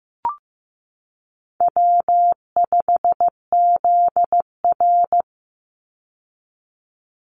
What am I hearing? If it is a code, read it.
Morse code, heard as a keyed tone, about 15 wpm, 700 Hz. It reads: W5ZR
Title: Easy Call Signs Repeated 3x - 15wpm